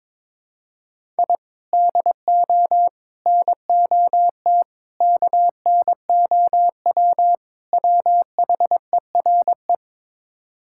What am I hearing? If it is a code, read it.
Morse code, heard as a keyed tone, about 22 wpm, 700 Hz. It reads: I DO NOT KNOW WHERE
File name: practice / 22wpm / Build A Phrase - 5-Word Phrases - 22wpm